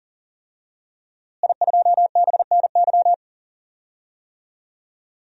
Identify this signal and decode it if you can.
Morse code, heard as a keyed tone, about 40 wpm, 700 Hz. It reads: I2BNY
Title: DXCC-100 Call Signs - 5 Characters - Repeated 3x - 40wpm